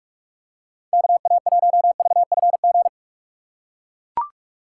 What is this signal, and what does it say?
Morse code, heard as a keyed tone, about 44 wpm, 700 Hz. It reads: KA1VFG
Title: USA Call Signs - 6 Characters - 6x Character Spacing - 45wpm